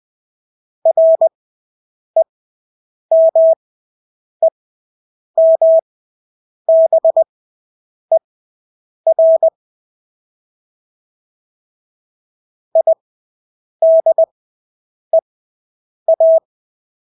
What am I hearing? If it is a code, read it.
Morse code, heard as a keyed tone, about 20 wpm, spaced as for 8 wpm, 650 Hz. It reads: REMEMBER IDEA
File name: Engelske ord 20/8 wpm